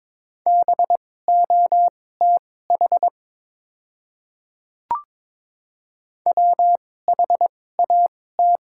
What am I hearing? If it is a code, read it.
Morse code, heard as a keyed tone, about 22 wpm, spaced as for 16 wpm, 700 Hz. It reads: BOTH WHAT
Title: 2x Character Spacing - Top 500 Words - 22wpm